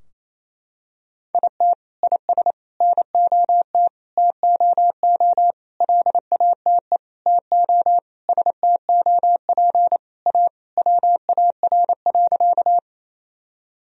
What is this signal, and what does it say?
Morse code, heard as a keyed tone, about 28 wpm, 700 Hz. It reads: IT IS NOT TOO LATE TO STOP A WAR.
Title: Sentences from Top 1000 Words - Review Part 2 - 28wpm